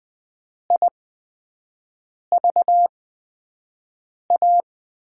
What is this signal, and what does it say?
Morse code, heard as a keyed tone, about 20 wpm, spaced as for 5 wpm, 700 Hz. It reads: IVA
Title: USA Call Signs - 6 Characters - 8x Character Spacing - 20wpm